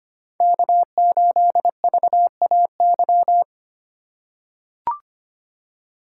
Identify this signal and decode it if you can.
Morse code, heard as a keyed tone, about 25 wpm, 700 Hz. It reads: K8VAY
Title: USA Call Signs - 5 Characters - 4x Character Spacing - 25wpm